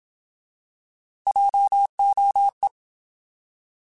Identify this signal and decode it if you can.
Morse code, heard as a keyed tone, about 27 wpm, 800 Hz. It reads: JOE
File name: Alphabetical / Alphabetical mp3s / Js @25